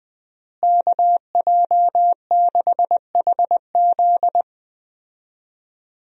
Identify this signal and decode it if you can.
Morse code, heard as a keyed tone, about 20 wpm, 700 Hz. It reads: KJ6HZ
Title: USA Call Signs - 5 Characters - Repeated 2x - 20wpm